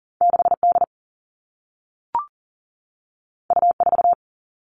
Morse code, heard as a keyed tone, about 38 wpm, 700 Hz. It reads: 6 D U 4
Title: Two Challenging Letter-Number Combinations 40wpm